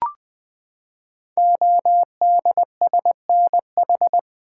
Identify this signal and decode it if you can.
Morse code, heard as a keyed tone, about 20 wpm, 700 Hz. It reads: ODSNH
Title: Code Groups - 5 Letter:Numbers - 20wpm